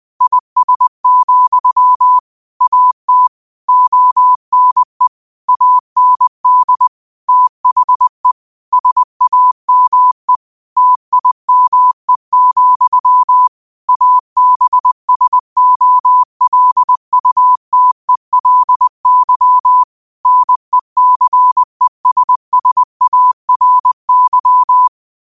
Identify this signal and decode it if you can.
Morse code, heard as a keyed tone, about 20 wpm, 1000 Hz. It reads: IS, AT ONE AND THE SAME TIME, ABSOLUTELY NECESSARY
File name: qotdfiles / 20wpm